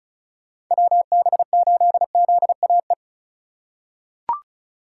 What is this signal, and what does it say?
Morse code, heard as a keyed tone, about 34 wpm, 700 Hz. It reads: WB8ZAE
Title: USA Call Signs - 6 Characters - 2x Character Spacing - 35wpm